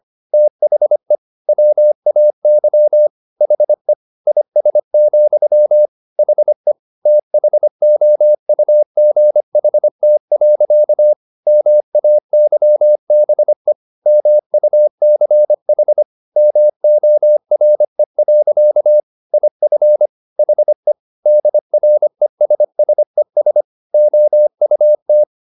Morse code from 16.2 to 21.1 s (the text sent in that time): MORE. IF HE